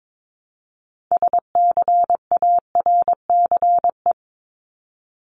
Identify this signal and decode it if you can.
Morse code, heard as a keyed tone, about 22 wpm, 700 Hz. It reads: SCARCE
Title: LY Suffix - Top 100 Words with Base Words 22wpm